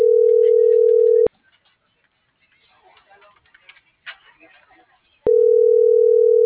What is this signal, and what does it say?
Ringback tone (North America), 440 + 480 Hz.